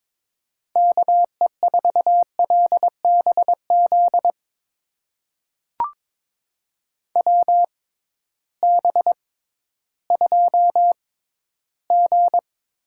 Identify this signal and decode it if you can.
Morse code, heard as a keyed tone, about 22 wpm, spaced as for 13 wpm, 700 Hz. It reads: KE4LBZ WB2G
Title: USA Call Signs - 6 Characters - 6x Character Spacing - 22wpm